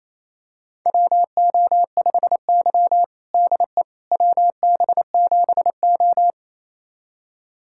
Morse code, heard as a keyed tone, about 28 wpm, 700 Hz. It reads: WO5Y DE WB7O
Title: USA Callsign DE Callsign - 4 Characters - 28wpm